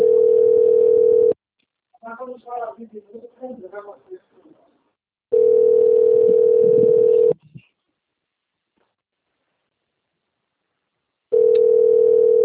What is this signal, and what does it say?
Ringback tone (North America), 440 + 480 Hz, with speech.
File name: out- - - - - .16964